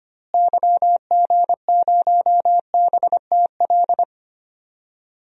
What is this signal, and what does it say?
Morse code, heard as a keyed tone, about 25 wpm, 700 Hz. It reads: YG0BTL
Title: DXCC-100 Call Signs - 6 Characters - Repeated 2x - 25wpm